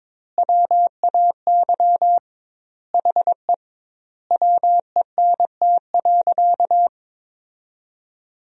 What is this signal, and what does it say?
Morse code, heard as a keyed tone, about 22 wpm, 700 Hz. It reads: WAY HE WENT.
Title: 2x Word-Spacing - Sentences from Top 200 Words - 22wpm